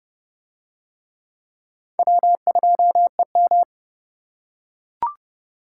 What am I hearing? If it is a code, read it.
Morse code, heard as a keyed tone, about 29 wpm, 700 Hz. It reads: W2EM